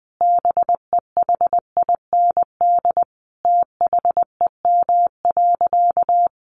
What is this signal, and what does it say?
Morse code, heard as a keyed tone, about 19 wpm, 700 Hz. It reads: BEHIND THEM.
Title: Sentences from Top 1400 Words 20wpm